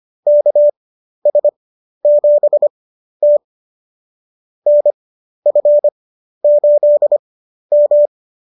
Morse code, heard as a keyed tone, about 25 wpm, spaced as for 12 wpm, 590 Hz. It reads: KS7T NF8M